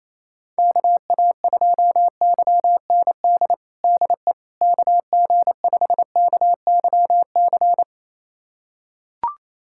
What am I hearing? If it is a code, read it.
Morse code, heard as a keyed tone, about 28 wpm, 700 Hz. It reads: KA2YND DE KG5KYC
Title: USA Callsign DE Callsign - 6 Characters - 28wpm